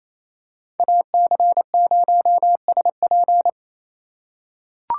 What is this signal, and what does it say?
Morse code, heard as a keyed tone, about 28 wpm, 700 Hz. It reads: AC0SP
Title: USA Call Signs - 4x Character Spacing - 28wpm